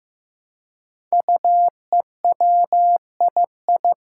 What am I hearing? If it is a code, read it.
Morse code, heard as a keyed tone, about 15 wpm, 700 Hz. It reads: UEWII
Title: Five Challenging Letter-Number Combinations 15wpm